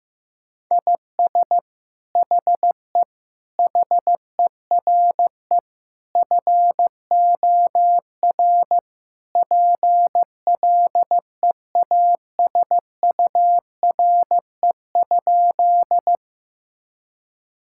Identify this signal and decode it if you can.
Morse code, heard as a keyed tone, about 15 wpm, 700 Hz. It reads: IS HE HERE FOR PLEASURE?
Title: Sentences from Top 1600 Words 15wpm